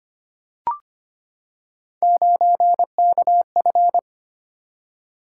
Morse code, heard as a keyed tone, about 25 wpm, 700 Hz. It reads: 9KF
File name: Three Letter-Number Combinations - 25wpm